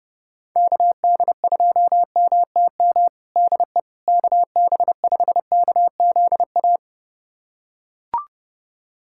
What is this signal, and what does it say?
Morse code, heard as a keyed tone, about 30 wpm, 700 Hz. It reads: KD2MTM DE KB5KZA
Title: USA Callsign DE Callsign - 6 Characters - 30wpm